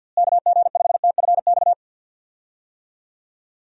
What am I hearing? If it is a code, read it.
Morse code, heard as a keyed tone, about 52 wpm, 700 Hz. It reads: KG5TFX
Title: USA Call Signs - 6 Characters - Repeated 2x - 50wpm